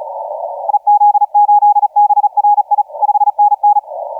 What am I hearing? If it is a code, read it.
Morse code, heard as a keyed tone, about 36 wpm, 830 Hz, with radio noise. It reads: EZ8BRI 5NN
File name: Z81R 1555UTC 15m